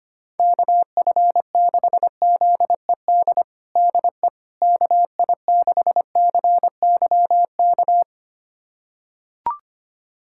Morse code, heard as a keyed tone, about 25 wpm, 700 Hz. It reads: KF6ZED DE KI6CYK
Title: USA Callsign DE Callsign - 6 Characters - 25wpm